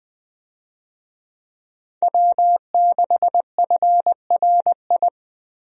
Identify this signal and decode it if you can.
Morse code, heard as a keyed tone, about 20 wpm, 700 Hz. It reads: W6FRI
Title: USA Call Signs - 5 Characters - Repeated 3x - 20wpm